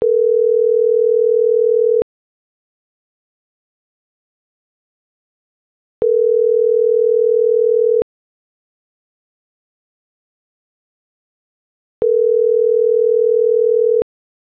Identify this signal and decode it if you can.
Ringback tone (North America), 440 + 480 Hz.
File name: CENTRAL GENERAL / out ricardo.diaz